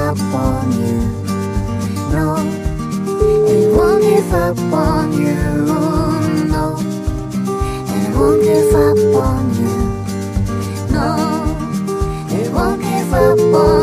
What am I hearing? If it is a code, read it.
Ringback tone (Europe, 425 Hz), with noise.